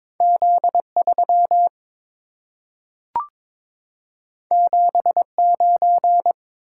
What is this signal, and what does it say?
Morse code, heard as a keyed tone, about 22 wpm, 690 Hz. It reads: Z3 79
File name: practice / 22wpm / Two Letter-Number Combinations - 22wpm